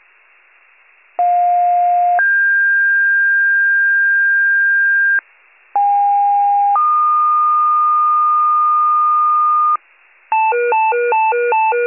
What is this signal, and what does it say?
Two-tone page, heard as a one-second tone, then a three-second tone, about 710 then 1600 Hz, about 810 then 1220 Hz, with noise. It is heard from 1.2 to 9.8 s.
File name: pager / 46 RESCUE DUTY / wcalert